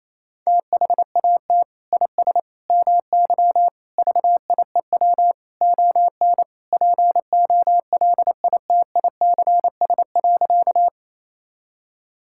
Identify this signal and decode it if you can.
Morse code, heard as a keyed tone, about 28 wpm, 700 Hz. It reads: THAT IS MY VIEW ON POLITICS.
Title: Sentences from Top 1000 Words - Review Part 6 - 28wpm